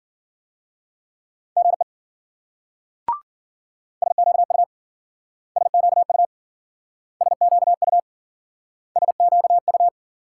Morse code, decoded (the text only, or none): GE SQU SQU SQU SQU